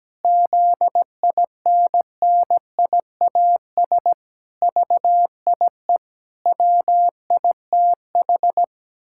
Morse code, heard as a keyed tone, about 17 wpm, 700 Hz. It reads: ZINNIAS VIE WITH
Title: Pangram Challenge v2 - 17wpm